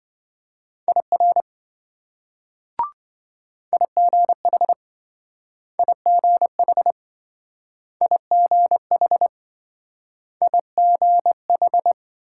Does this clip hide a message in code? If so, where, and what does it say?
Yes: Morse code from 0.9 to 11.9 s, heard as a keyed tone, about 27 wpm, 700 Hz. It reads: IR IGH IGH IGH IGH